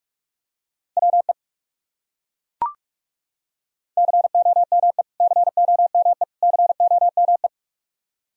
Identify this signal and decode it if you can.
Morse code, heard as a keyed tone, about 42 wpm, 700 Hz. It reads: WE COME COME COME
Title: Top 100 Words Repeated 45wpm